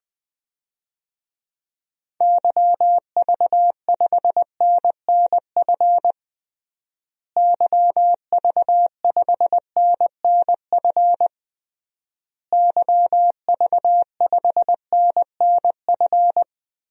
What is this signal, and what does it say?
Morse code, heard as a keyed tone, about 21 wpm, 700 Hz. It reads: YV5NNF YV5NNF YV5NNF
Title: DXCC-100 Call Signs - 6 Characters - Repeated 3x - 20wpm